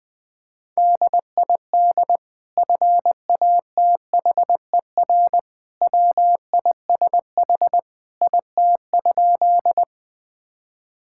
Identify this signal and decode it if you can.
Morse code, heard as a keyed tone, about 20 wpm, 700 Hz. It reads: DID FATHER WISH IT?
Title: Sentences from Top 1600 Words 20wpm